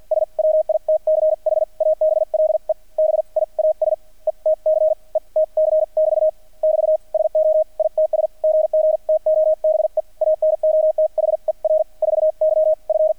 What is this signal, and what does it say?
Morse code, heard as a keyed tone, about 48 wpm, spaced as for 40 wpm, 630 Hz. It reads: SWITCHABLE BIAS ETC ETC <BT> <BT> SO ITS GG TO BE ANOTHER 4Y F
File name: archived / CRSnet